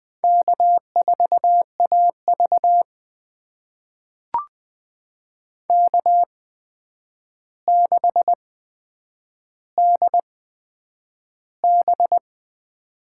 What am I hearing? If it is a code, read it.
Morse code, heard as a keyed tone, about 21 wpm, spaced as for 9 wpm, 700 Hz. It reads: K4AV K6DB
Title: USA Call Signs - 4 Characters - 8x Character Spacing - 20wpm